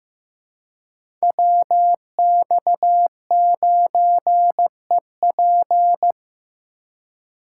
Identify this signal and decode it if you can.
Morse code, heard as a keyed tone, about 15 wpm, 700 Hz. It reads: WX9EP
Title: USA Call Signs - 5 Characters - 4x Character Spacing - 15wpm